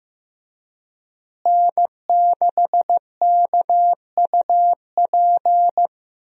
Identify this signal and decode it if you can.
Morse code, heard as a keyed tone, about 15 wpm, 700 Hz. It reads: N6KUP
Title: USA Call Signs - 5 Characters - Repeated 3x - 15wpm